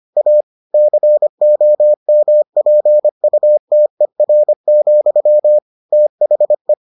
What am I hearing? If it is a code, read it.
Morse code, heard as a keyed tone, about 25 wpm, 600 Hz. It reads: A COMPUTER, THE